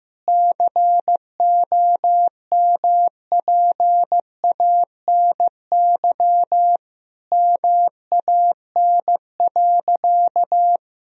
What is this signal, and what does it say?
Morse code, heard as a keyed tone, about 15 wpm, 700 Hz. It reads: COMPANY MAN.
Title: Sentences from Top 1000 Words - Review Part 3 - 15wpm